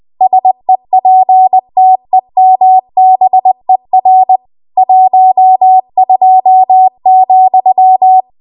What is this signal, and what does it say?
Morse code, heard as a keyed tone, about 20 wpm, 750 Hz. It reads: SEPTEMBER 12,
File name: Archive / 20 WPM / 20WPM